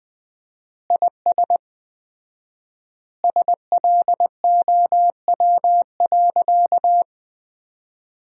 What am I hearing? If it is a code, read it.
Morse code, heard as a keyed tone, about 20 wpm, 700 Hz. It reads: IS SLOW.
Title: 4x Word-Spacing - Sentences from Top 400 Words - 20wpm